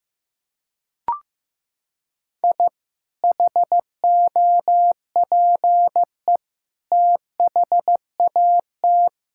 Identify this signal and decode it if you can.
Morse code, heard as a keyed tone, about 15 wpm, 700 Hz. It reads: I HOPE THAT
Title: Sentences from Top 1000 Words - Review Part 5 - 15wpm